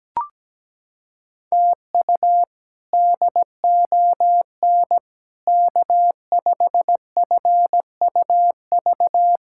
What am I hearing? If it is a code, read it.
Morse code, heard as a keyed tone, about 17 wpm, 700 Hz. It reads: TU DON K5FUV
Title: K1USN SST QSO Practice - 17wpm